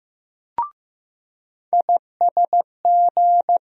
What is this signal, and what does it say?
Morse code, heard as a keyed tone, about 15 wpm, 700 Hz. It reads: ISG
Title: USA Call Sign Postfixes - 3 Letters 15wpm